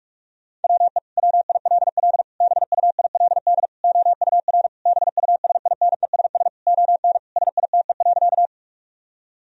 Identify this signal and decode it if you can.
Morse code, heard as a keyed tone, about 41 wpm, 700 Hz. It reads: WE WILL BUILD OUR BUSINESS ON SITE.
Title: Sentences from Top 1000 Words - Review Part 4 - 45wpm